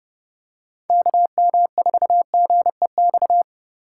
Morse code, heard as a keyed tone, about 30 wpm, 700 Hz. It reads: KM4GEX